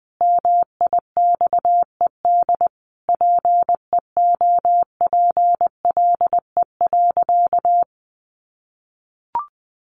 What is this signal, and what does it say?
Morse code, heard as a keyed tone, about 19 wpm, 700 Hz. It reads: MIXED PEOPLE.
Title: Sentences from Top 1500 Words 20wpm